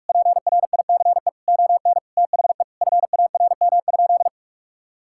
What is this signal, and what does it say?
Morse code, heard as a keyed tone, about 46 wpm, 700 Hz. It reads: PRICE ON THE FARM?
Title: Sentences from Top 1000 Words - Review Part 6 - 45wpm